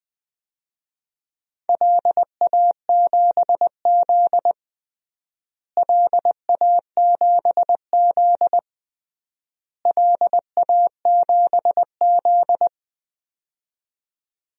Morse code, heard as a keyed tone, about 20 wpm, 700 Hz. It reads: LA7Z LA7Z LA7Z
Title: DXCC-100 Call Signs - 4 Characters - Repeated 3x - 20wpm